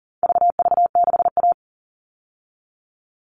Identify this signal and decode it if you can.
Morse code, heard as a keyed tone, about 40 wpm, 700 Hz. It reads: VV6A
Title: All Cut Numbers - 4-Digit 40wpm